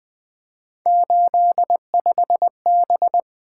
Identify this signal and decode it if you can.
Morse code, heard as a keyed tone, about 20 wpm, 700 Hz. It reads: 85B
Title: Three Letter-Number Combinations - 20wpm